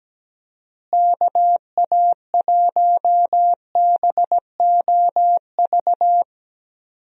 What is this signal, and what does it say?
Morse code, heard as a keyed tone, about 17 wpm, 700 Hz. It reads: KA1BOV
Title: USA Call Signs - 6 Characters - Repeated 3x - 17wpm